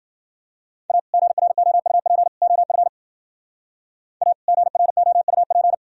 Morse code, heard as a keyed tone, about 51 wpm, 700 Hz. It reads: A GROUP OF A GROUP